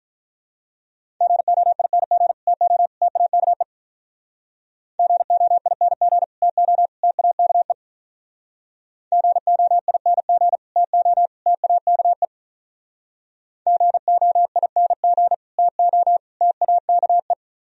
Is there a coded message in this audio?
Yes: Morse code from 1.2 to 17.3 s, heard as a keyed tone, about 46 wpm, 700 Hz. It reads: GOING TO TAKE GOING TO TAKE GOING TO TAKE GOING TO TAKE